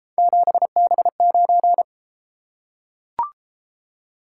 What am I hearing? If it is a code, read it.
Morse code, heard as a keyed tone, about 29 wpm, 700 Hz. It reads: 7B9